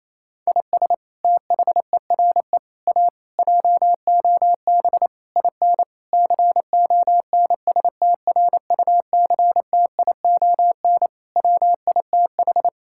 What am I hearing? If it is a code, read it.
Morse code, heard as a keyed tone, about 28 wpm, 700 Hz. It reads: IS THERE A JOB IN CONSTRUCTION WITH
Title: Sentences from Top 1600 Words 28wpm